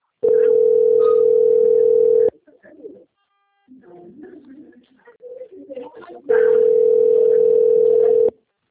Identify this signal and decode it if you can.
Ringback tone (North America), 440 + 480 Hz, with speech.
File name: out- - - - - .6556